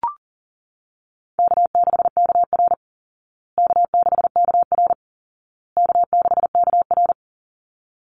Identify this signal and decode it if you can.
Morse code, heard as a keyed tone, about 40 wpm, 700 Hz. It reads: K6KR K6KR K6KR
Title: Easy Call Signs Repeated 3x - 40wpm